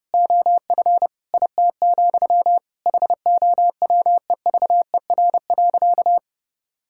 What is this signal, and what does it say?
Morse code, heard as a keyed tone, about 30 wpm, 700 Hz. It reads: OF IT, HOWEVER.
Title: Sentences from Top 1000 Words - Review Part 6 - 30wpm